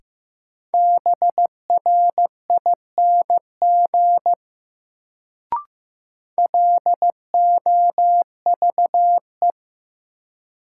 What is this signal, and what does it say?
Morse code, heard as a keyed tone, about 15 wpm, 700 Hz. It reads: BRING LOVE
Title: Sets of 5 Words Encore - New Word Drill - 15wpm